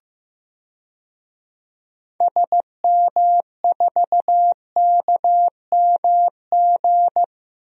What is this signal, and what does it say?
Morse code, heard as a keyed tone, about 15 wpm, 700 Hz. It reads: SM4KMG
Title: DXCC-100 Call Signs - 6 Characters - Repeated 3x - 15wpm